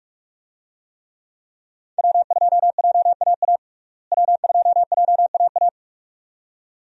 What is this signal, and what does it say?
Morse code, heard as a keyed tone, about 46 wpm, 700 Hz. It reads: W2JAA W2JAA
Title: USA Call Signs - 5 Characters - Repeated 2x - 45wpm